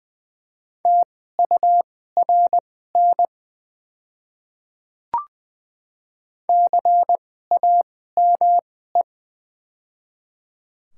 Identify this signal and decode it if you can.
Morse code, heard as a keyed tone, about 20 wpm, spaced as for 14 wpm, 700 Hz. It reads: TURN CAME
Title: 2x Character Spacing - Top 300 Words - 20wpm